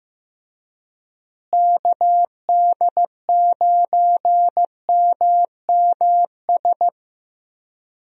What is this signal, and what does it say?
Morse code, heard as a keyed tone, about 15 wpm, 700 Hz. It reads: KD9MMS
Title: USA Call Signs - 6 Characters - Repeated 3x - 15wpm